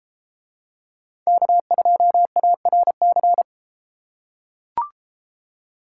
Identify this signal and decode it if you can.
Morse code, heard as a keyed tone, about 32 wpm, 700 Hz. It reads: K2ARC